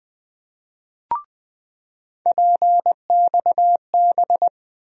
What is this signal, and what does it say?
Morse code, heard as a keyed tone, about 21 wpm, 700 Hz. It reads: PXB
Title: USA Call Sign Postfixes - 3 Letters 20wpm